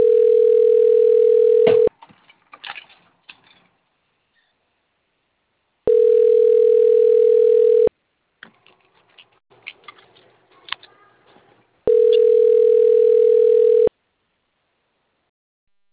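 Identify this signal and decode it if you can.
Ringback tone (North America), 440 + 480 Hz.
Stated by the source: Outgoing telephone call recorded on the Limeira IV system.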